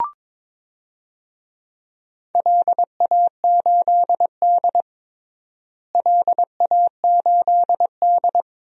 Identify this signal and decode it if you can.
Morse code, heard as a keyed tone, about 21 wpm, 700 Hz. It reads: LA8D LA8D
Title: DXCC-100 Call Signs - 4 Characters - Repeated 2x - 22wpm